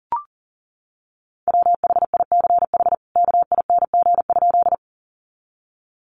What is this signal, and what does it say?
Morse code, heard as a keyed tone, about 40 wpm, 700 Hz. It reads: WHICH KING?
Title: Sentences from Top 400 Words 40wpm